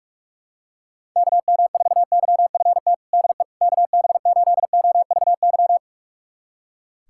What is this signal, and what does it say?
Morse code, heard as a keyed tone, about 43 wpm, 700 Hz. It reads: KM4YFT DE KB8OVY
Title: USA Callsign DE Callsign - 6 Characters - 45wpm